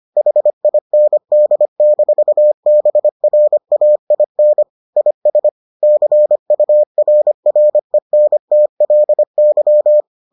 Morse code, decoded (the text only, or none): HIND-BRAIN IS CURRENTLY